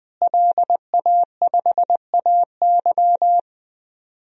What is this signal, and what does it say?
Morse code, heard as a keyed tone, about 20 wpm, 700 Hz. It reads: LA5AY
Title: DXCC-100 Call Signs - 5 Characters - Repeated 2x - 20wpm